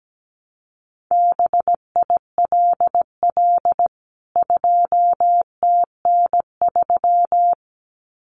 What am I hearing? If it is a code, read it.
Morse code, heard as a keyed tone, about 17 wpm, 700 Hz. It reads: BILL 2TN3
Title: CWT Exchange 17wpm